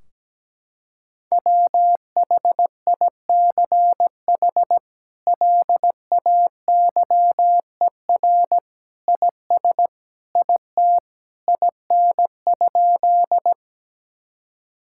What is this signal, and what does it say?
Morse code, heard as a keyed tone, about 17 wpm, 700 Hz. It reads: WHICH LAYER IS IT IN?
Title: Sentences from Top 1600 Words 17wpm